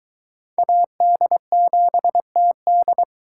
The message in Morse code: AD7TD

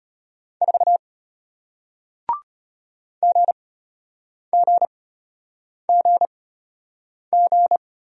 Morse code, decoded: 4 GGGG

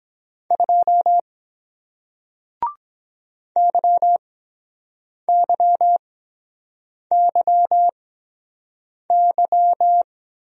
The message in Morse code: 2 YYYY

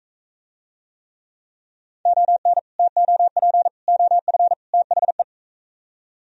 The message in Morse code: ON TOP OF THE

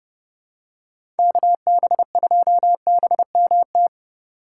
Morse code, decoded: KB2BMT

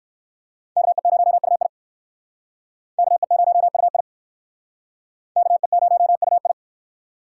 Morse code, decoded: KE0UI KE0UI KE0UI

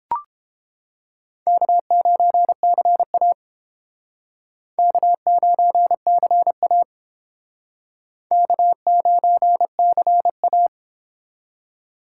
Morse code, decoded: K9CA K9CA K9CA

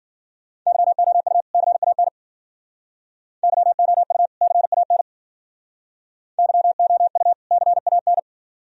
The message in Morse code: YOU CAN YOU CAN YOU CAN

